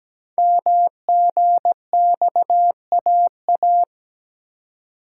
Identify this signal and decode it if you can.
Morse code, heard as a keyed tone, about 17 wpm, 700 Hz. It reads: MGXAA
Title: Code Groups - 5 Letters - 17wpm